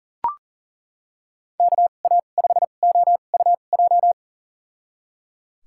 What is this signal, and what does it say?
Morse code, heard as a keyed tone, about 38 wpm, spaced as for 28 wpm, 700 Hz. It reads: KA5OUJ
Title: USA Call Signs - 6 Characters - 2x Character Spacing - 40wpm